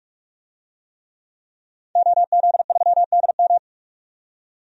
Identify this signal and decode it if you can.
Morse code, heard as a keyed tone, about 46 wpm, 700 Hz. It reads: OZ3DM
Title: DXCC-100 Call Signs - 5 Characters - 45wpm